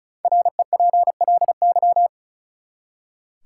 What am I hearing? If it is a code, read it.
Morse code, heard as a keyed tone, about 36 wpm, 700 Hz. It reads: REPLY
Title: Five Letter Words - 35wpm